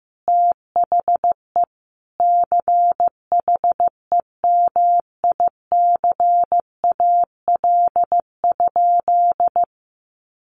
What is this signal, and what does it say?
Morse code, heard as a keyed tone, about 15 wpm, 700 Hz. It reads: THE CHEMICAL?
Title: Sentences from Top 1200 Words 15wpm